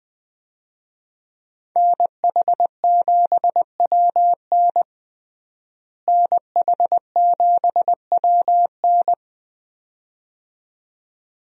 Morse code, heard as a keyed tone, about 20 wpm, 700 Hz. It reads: NH7WN NH7WN
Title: USA Call Signs - 5 Characters - Repeated 2x - 20wpm